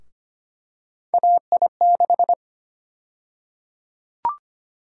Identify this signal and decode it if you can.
Morse code, heard as a keyed tone, about 25 wpm, 700 Hz. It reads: AI6
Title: USA Call Sign Prefixes - 25wpm